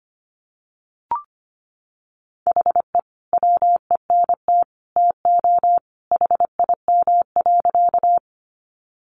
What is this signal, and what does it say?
Morse code, heard as a keyed tone, about 25 wpm, 700 Hz. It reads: HE WENT TO HIM.
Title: Sentences from Top 200 Words 25wpm